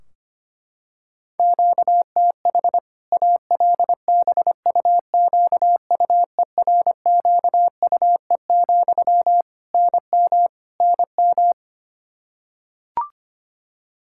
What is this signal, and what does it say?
Morse code, heard as a keyed tone, about 25 wpm, 700 Hz. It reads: QTH ALBUQUERQUE, NM NM
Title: Practice QSO - Exchange 1 - Beginning - 25wpm